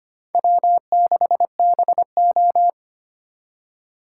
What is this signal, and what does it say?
Morse code, heard as a keyed tone, about 32 wpm, 700 Hz. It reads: W6BO